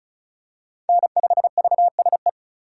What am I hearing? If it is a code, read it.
Morse code, heard as a keyed tone, about 34 wpm, 700 Hz. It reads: N5VSE
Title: USA Call Signs - 5 Characters - 4x Character Spacing - 35wpm